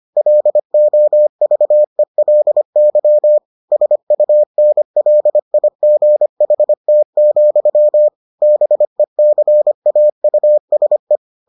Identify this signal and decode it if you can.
Morse code, heard as a keyed tone, about 25 wpm, 600 Hz. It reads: LOVELY SUNLIGHT, BECAUSE